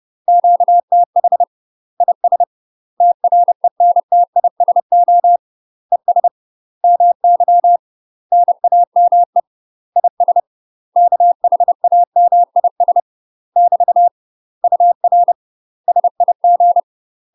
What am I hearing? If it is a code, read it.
Morse code, heard as a keyed tone, about 30 wpm, 700 Hz. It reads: QTH IS TRENTISO ES MY NAME IS KHAMIS <BT> UR SIG
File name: UK MASTER QSOs TRAINING / 30WPM / UK MASTER QSOs 0016